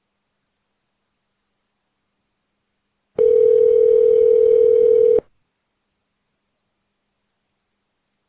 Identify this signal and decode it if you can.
Ringback tone (North America), 440 + 480 Hz.